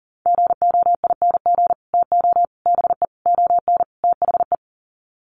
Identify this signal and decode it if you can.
Morse code, heard as a keyed tone, about 40 wpm, 700 Hz. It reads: GOING TO BE ON THE
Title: Sets of 5 Words Encore2 40wpm